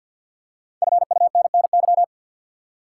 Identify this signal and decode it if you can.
Morse code, heard as a keyed tone, about 45 wpm, 700 Hz. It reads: FUNNY
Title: Five Letter Words - 50wpm